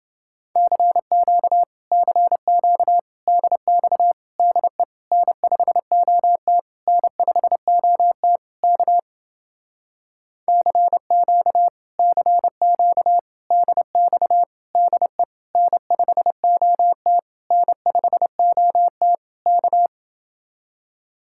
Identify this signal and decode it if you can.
Morse code, heard as a keyed tone, about 30 wpm, 700 Hz. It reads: CQ CQ DX DE N5OT N5OT K CQ CQ DX DE N5OT N5OT K